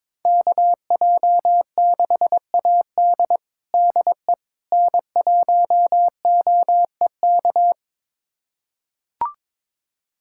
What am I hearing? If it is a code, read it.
Morse code, heard as a keyed tone, about 22 wpm, 700 Hz. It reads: KJ6AD DE N1OEK
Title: USA Callsign DE Callsign - 5 Characters - 22wpm